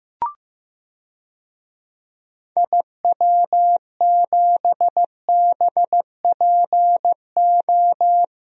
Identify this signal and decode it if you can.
Morse code, heard as a keyed tone, about 15 wpm, 700 Hz. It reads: IW7BPO
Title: DXCC-100 Call Signs - 6 Characters - 15wpm